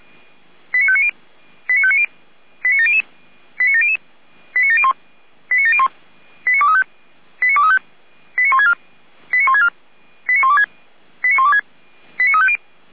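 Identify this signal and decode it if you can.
Five-tone selective call (ZVEI), heard as quick runs of five tones, tones of about 70 ms. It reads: 89590 89590 89700 89700 89701 89701 89235 89235 89164 89164 89126 89126 89350
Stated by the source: ZVEI five-tone call received by SDR.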